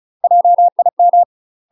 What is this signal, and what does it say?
Morse code, heard as a keyed tone, about 36 wpm, 700 Hz. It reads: JIM
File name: UK MASTER QSOs TRAINING / 35WPM / UK MASTER QSOs 0002